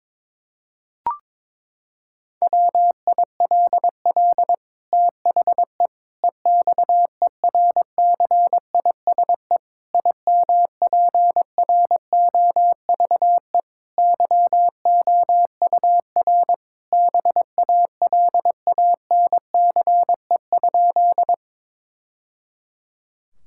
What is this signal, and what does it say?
Morse code, heard as a keyed tone, about 22 wpm, 700 Hz. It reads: WILL THE EXERCISE IMPROVE YOUR BALANCE?
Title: Sentences from Top 1000 Words - Review Part 6 - 22wpm